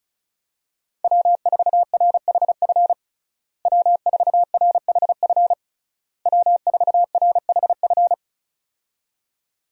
Morse code, heard as a keyed tone, about 35 wpm, 700 Hz. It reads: W4RHF W4RHF W4RHF
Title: USA Call Signs - 5 Characters - Repeated 3x - 35wpm